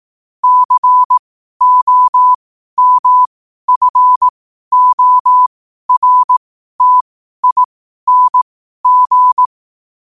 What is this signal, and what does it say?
Morse code, heard as a keyed tone, about 18 wpm, spaced as for 12 wpm, 1000 Hz. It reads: COMFORTING
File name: qotdfiles / 12wpm